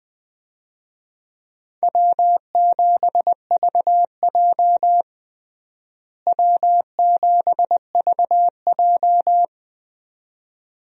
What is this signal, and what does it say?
Morse code, heard as a keyed tone, about 20 wpm, 700 Hz. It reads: W7VJ W7VJ
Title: USA Call Signs - 4 Characters - Repeated 2x - 20wpm